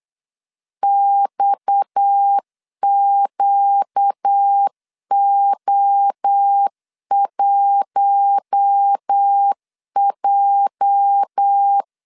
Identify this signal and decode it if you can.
Morse code, heard as a keyed tone, about 8 wpm, 800 Hz. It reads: XQO1J
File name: LEZIONI / 8 YJ45 / 8 YJ 8wpm 3313